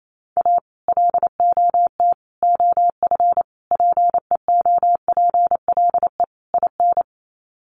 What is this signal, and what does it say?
Morse code, heard as a keyed tone, about 28 wpm, 700 Hz. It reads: A LOT OF PEOPLE IN
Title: Sets of 5 Words 28wpm